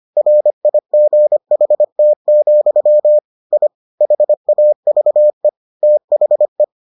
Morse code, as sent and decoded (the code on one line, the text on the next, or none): .-. .. --. .... - --..-- / .. / .... .- ...- . / - .... .
RIGHT, I HAVE THE